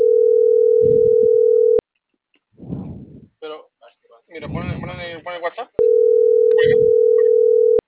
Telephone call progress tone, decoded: ringback tone

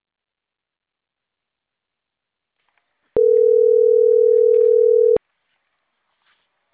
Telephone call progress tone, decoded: ringback tone

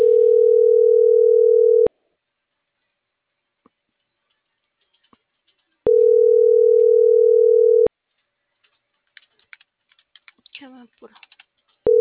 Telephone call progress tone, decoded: ringback tone

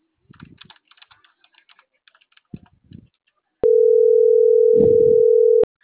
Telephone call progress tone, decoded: ringback tone

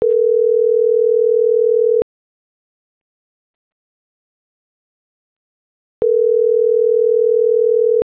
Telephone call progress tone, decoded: ringback tone

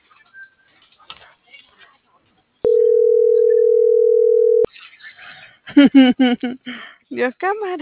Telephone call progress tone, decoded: ringback tone

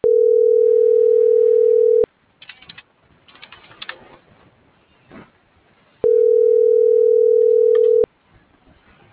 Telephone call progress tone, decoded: ringback tone

